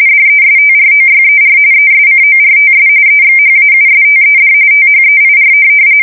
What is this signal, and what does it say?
RTTY, about 45.45 baud. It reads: DESTROYER ESCORTS EXHAUSTED THEIR
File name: fsk170 / USS-New-Jersey